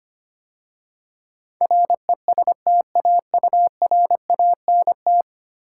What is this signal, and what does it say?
Morse code, heard as a keyed tone, about 25 wpm, 700 Hz. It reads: RESTAURANT
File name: 10 Letter Words - First 3 Letters Given - 25wpm